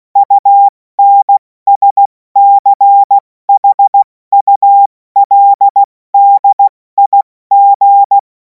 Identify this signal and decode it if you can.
Morse code, heard as a keyed tone, about 16 wpm, 800 Hz. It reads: UNSCHULDIG
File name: Morsekurs / Witze / Witz 10. Februar 2026 70 F80